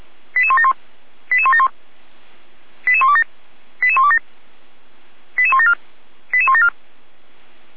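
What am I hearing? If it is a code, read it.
Five-tone selective call (ZVEI), heard as quick runs of five tones, tones of about 70 ms. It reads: 88171 88171 88127 88127 88174 88174
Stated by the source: ZVEI five-tone selective call 88167 received by SDR.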